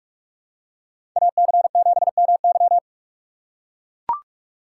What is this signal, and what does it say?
Morse code, heard as a keyed tone, about 42 wpm, 700 Hz. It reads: AC7MY
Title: USA Call Signs - 5 Characters - 3x Character Spacing - 45wpm